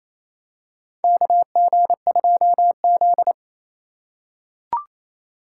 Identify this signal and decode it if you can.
Morse code, heard as a keyed tone, about 28 wpm, 700 Hz. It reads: KG2Z
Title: USA Call Signs - 4 Characters - 3x Character Spacing - 28wpm